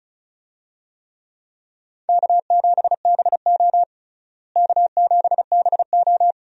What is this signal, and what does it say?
Morse code, heard as a keyed tone, about 34 wpm, 700 Hz. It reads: K7BO K7BO
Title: USA Call Signs - 4 Characters - Repeated 3x - 35wpm